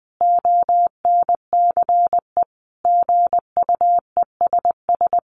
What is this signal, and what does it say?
Morse code, heard as a keyed tone, about 20 wpm, 700 Hz. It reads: ONCE GUESS
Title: Sentences from Top 1500 Words 20wpm